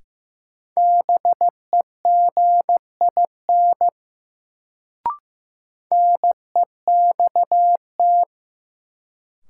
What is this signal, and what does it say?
Morse code, heard as a keyed tone, about 15 wpm, 700 Hz. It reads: BEGIN NEXT
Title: Sentences from Top 300 Words - New Word Drill - 15wpm